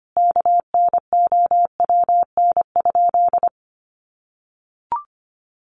Morse code, heard as a keyed tone, about 25 wpm, 700 Hz. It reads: KNOWN?
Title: Sentences from Top 1300 Words 25wpm